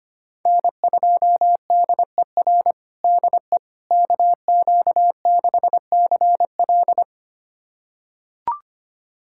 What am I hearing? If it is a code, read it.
Morse code, heard as a keyed tone, about 25 wpm, 700 Hz. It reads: N2DER DE KQ6CL
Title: USA Callsign DE Callsign - 5 Characters - 25wpm